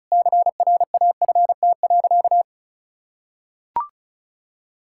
Morse code, heard as a keyed tone, about 34 wpm, 700 Hz. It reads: CRAFT.
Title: Sentences from Top 1000 Words - Review Part 6 - 35wpm